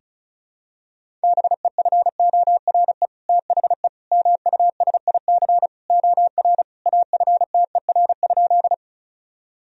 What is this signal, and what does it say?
Morse code, heard as a keyed tone, about 36 wpm, 700 Hz. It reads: BEFORE THE MUSIC OR AFTER?
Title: Sentences from Top 1000 Words - Review Part 1 - 35wpm